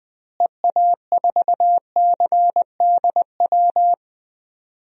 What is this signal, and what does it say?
Morse code, heard as a keyed tone, about 20 wpm, 700 Hz. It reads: EA4CDW
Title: DXCC-100 Call Signs - 6 Characters - Repeated 3x - 20wpm